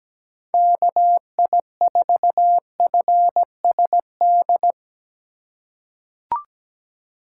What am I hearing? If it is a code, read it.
Morse code, heard as a keyed tone, about 17 wpm, 700 Hz. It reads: KI4FSD
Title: USA Call Signs - 6 Characters - 4x Character Spacing - 17wpm